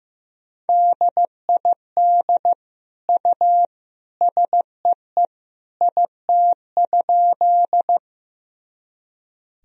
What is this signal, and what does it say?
Morse code, heard as a keyed tone, about 16 wpm, 700 Hz. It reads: DID U SEE IT?
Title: Sentences from Top 100 Words and Abbreviations 15wpm